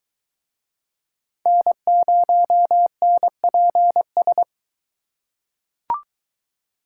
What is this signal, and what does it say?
Morse code, heard as a keyed tone, about 23 wpm, 700 Hz. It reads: N0NPS